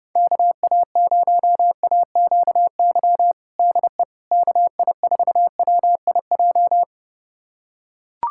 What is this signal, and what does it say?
Morse code, heard as a keyed tone, about 30 wpm, 700 Hz. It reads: KA0AQY DE KI4WIJ
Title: USA Callsign DE Callsign - 6 Characters - 30wpm